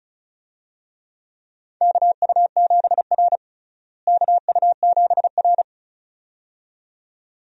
Morse code, heard as a keyed tone, about 36 wpm, 700 Hz. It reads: KU7R KU7R
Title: USA Call Signs - 4 Characters - Repeated 2x - 35wpm